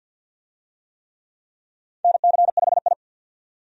Morse code, heard as a keyed tone, about 52 wpm, 700 Hz. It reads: NC5I
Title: USA Call Signs - 4 Characters - 50wpm